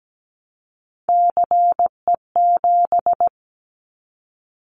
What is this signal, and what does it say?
Morse code, heard as a keyed tone, about 17 wpm, 700 Hz. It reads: CE7
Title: Callsign Prefixes for Top 100 DXCC - 17wpm